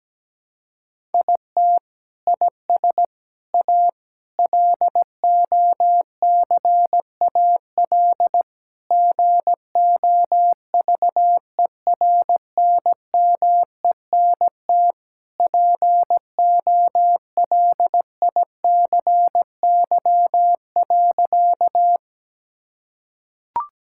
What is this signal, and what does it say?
Morse code, heard as a keyed tone, about 17 wpm, 700 Hz. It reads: IT IS A LOCAL GOVERNMENT POLICY.
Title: Sentences from Top 1000 Words - Review Part 5 - 17wpm